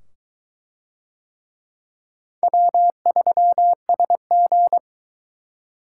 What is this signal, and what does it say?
Morse code, heard as a keyed tone, about 20 wpm, 700 Hz. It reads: W3SG